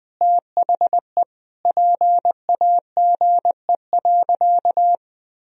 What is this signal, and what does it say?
Morse code, heard as a keyed tone, about 20 wpm, 700 Hz. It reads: THE PAGE.
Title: Sentences from Top 1000 Words - Review Part 1 - 20wpm